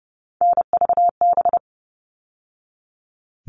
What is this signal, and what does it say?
Morse code, heard as a keyed tone, about 30 wpm, 700 Hz. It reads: NVB
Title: All Cut Numbers - 3-Digit 30wpm